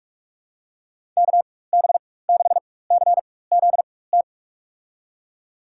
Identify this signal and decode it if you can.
Morse code, heard as a keyed tone, about 46 wpm, spaced as for 21 wpm, 700 Hz. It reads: KB6CZT
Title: USA Call Signs - 6 Characters - 4x Character Spacing - 45wpm